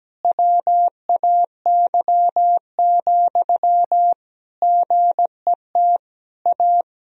Morse code, heard as a keyed tone, about 17 wpm, 700 Hz. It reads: WAY, GET A
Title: Sentences from Top 1000 Words - Review Part 1 - 17wpm